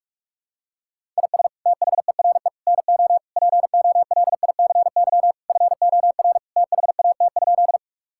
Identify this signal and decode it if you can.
Morse code, heard as a keyed tone, about 44 wpm, 700 Hz. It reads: IS THERE NO POLICY FOR THAT?
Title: Sentences from Top 1000 Words - Review Part 5 - 45wpm